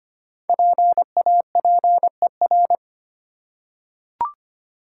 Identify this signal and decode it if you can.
Morse code, heard as a keyed tone, about 25 wpm, 700 Hz. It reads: PAPER
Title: Five Letter Words - 25wpm